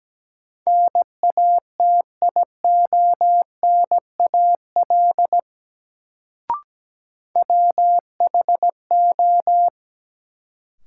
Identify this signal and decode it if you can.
Morse code, heard as a keyed tone, about 17 wpm, 700 Hz. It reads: NATIONAL WHO
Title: Sets of 2 Words - New Word Drill - 17wpm